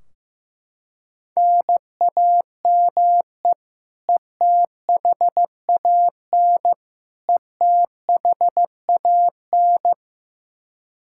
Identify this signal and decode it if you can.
Morse code, heard as a keyed tone, about 15 wpm, 700 Hz. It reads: NAME ETHAN ETHAN
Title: Practice QSO - Exchange 1 - Beginning - 15wpm